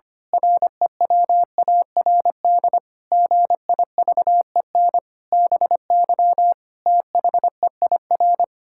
Morse code, heard as a keyed tone, about 25 wpm, 700 Hz. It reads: REWARD GIVEN BY THEIR
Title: Pangram Challenge v2 - 25wpm